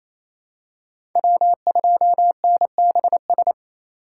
Morse code, heard as a keyed tone, about 28 wpm, 700 Hz. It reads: W2NBS
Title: USA Call Signs - 4x Character Spacing - 28wpm